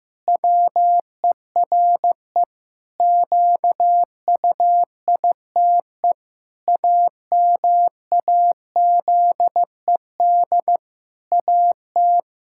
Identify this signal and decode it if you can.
Morse code, heard as a keyed tone, about 15 wpm, 700 Hz. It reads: WERE QUITE AMAZED AT
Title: Pangram Challenge v2 - 15wpm